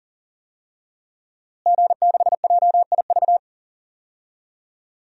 Morse code, heard as a keyed tone, about 40 wpm, 700 Hz. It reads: G6JIV
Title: DXCC-100 Call Signs - 5 Characters - Repeated 3x - 40wpm